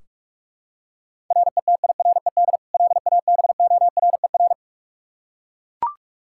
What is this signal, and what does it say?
Morse code, heard as a keyed tone, about 41 wpm, 700 Hz. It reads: RETIRED LABORER
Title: Practice QSO - Exchange 3 - More Info - 45wpm